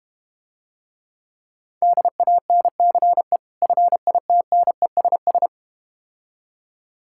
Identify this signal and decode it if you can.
Morse code, heard as a keyed tone, about 32 wpm, 700 Hz. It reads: DANCE FITNESS